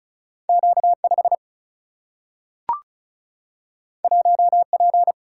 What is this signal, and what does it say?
Morse code, heard as a keyed tone, about 36 wpm, 700 Hz. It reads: Q5 1P
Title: Single-Letter Single-Number 35wpm